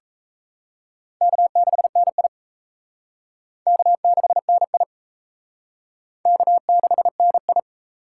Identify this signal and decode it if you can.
Morse code, heard as a keyed tone, about 36 wpm, 700 Hz. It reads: K6NI K6NI K6NI